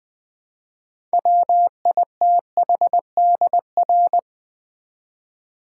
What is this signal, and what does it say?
Morse code, heard as a keyed tone, about 20 wpm, 700 Hz. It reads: WITHDR
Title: Build A Word - 2 Letters At A Time - 8-Letter Words - 20wpm